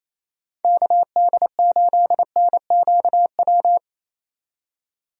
Morse code, heard as a keyed tone, about 28 wpm, 700 Hz. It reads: KD8NQW
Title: USA Call Signs - 6 Characters - 28wpm